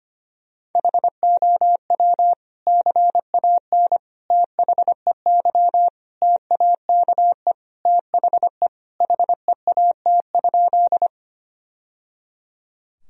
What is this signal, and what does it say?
Morse code, heard as a keyed tone, about 25 wpm, 700 Hz. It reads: HOW CAN THEY TAKE THE HEAT?
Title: Sentences from Top 1000 Words - Review Part 3 - 25wpm